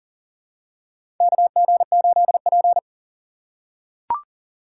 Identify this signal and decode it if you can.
Morse code, heard as a keyed tone, about 40 wpm, 700 Hz. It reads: KG8P
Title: USA Call Signs - 4 Characters - 2x Character Spacing - 40wpm